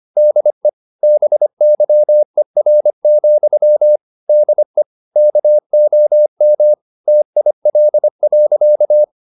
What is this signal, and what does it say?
Morse code, heard as a keyed tone, about 25 wpm, 600 Hz. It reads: DE BYER, DE KOM TIL.
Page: Ferie 25 wpm